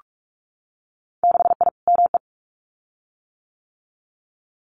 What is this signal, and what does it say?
Morse code, heard as a keyed tone, about 46 wpm, 700 Hz. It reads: 6I ME
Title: Winter Field Day Exchanges Encore 45wpm